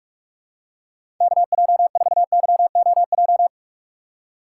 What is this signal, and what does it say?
Morse code, heard as a keyed tone, about 44 wpm, 700 Hz. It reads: KJ4YOJ
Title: USA Call Signs - 6 Characters - 6x Character Spacing - 45wpm